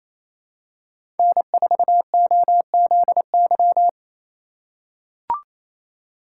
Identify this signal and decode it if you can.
Morse code, heard as a keyed tone, about 28 wpm, 700 Hz. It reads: N4OZY
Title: USA Call Signs - 5 Characters - 2x Character Spacing - 28wpm